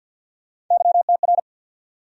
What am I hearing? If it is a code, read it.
Morse code, heard as a keyed tone, about 47 wpm, 700 Hz. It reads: YTR
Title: USA Call Sign Postfixes - 3 Letters 50wpm